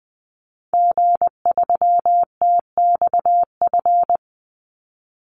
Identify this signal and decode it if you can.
Morse code, heard as a keyed tone, about 20 wpm, 700 Hz. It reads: G3TXF
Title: Call Signs Repeated 3x - 20wpm